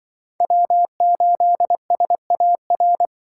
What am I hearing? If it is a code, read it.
Morse code, heard as a keyed tone, about 24 wpm, 700 Hz. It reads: W8SAR
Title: Speed Racing - USA 1x3 Call Signs 20wpm